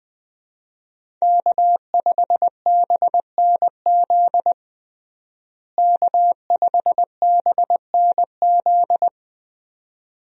Morse code, heard as a keyed tone, about 20 wpm, 700 Hz. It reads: K5BNZ K5BNZ
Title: USA Call Signs - 5 Characters - Repeated 2x - 20wpm